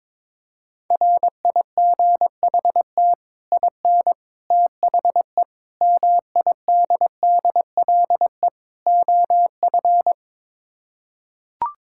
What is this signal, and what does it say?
Morse code, heard as a keyed tone, about 22 wpm, 700 Hz. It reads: RIGHT IN THE MIDDLE OF
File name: practice / 22wpm / Build A Phrase - 5-Word Phrases - 22wpm